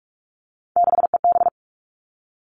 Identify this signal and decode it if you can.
Morse code, heard as a keyed tone, about 46 wpm, 700 Hz. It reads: 6EB
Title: Three Challenging Letter-Number Combinations 45wpm